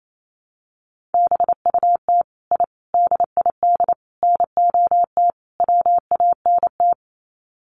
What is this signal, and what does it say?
Morse code, heard as a keyed tone, about 28 wpm, 700 Hz. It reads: BUT I DID NOT WANT
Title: Sets of 5 Words Encore 28wpm